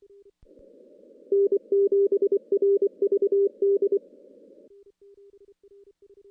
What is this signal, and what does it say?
Morse code, heard as a keyed tone, about 24 wpm, 400 Hz. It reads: N7RVD
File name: contests / CW-SS-k5tr- -2300